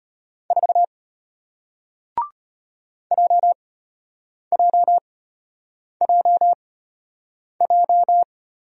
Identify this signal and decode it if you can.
Morse code, heard as a keyed tone, about 33 wpm, spaced as for 8 wpm, 700 Hz. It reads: 4 JJJJ